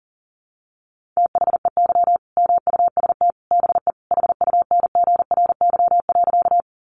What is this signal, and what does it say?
Morse code, heard as a keyed tone, about 40 wpm, 700 Hz. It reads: THEY MUST BE HUNGRY.
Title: Sentences from Top 1400 Words 40wpm